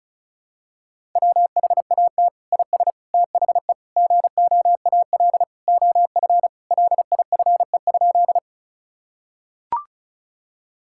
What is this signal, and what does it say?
Morse code, heard as a keyed tone, about 34 wpm, 700 Hz. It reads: WHAT IS THE GOAL OF LIFE?
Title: Sentences from Top 1000 Words - Review Part 5 - 35wpm